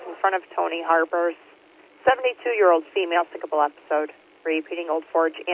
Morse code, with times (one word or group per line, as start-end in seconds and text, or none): none